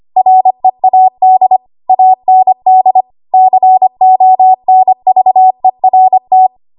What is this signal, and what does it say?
Morse code, heard as a keyed tone, about 25 wpm, 750 Hz. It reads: READ AND CONVERT
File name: Archive / 25 WPM / 25WPM